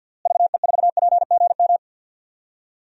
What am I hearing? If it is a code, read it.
Morse code, heard as a keyed tone, about 52 wpm, 700 Hz. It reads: VE4PGM
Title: DXCC-100 Call Signs - 6 Characters - Repeated 2x - 50wpm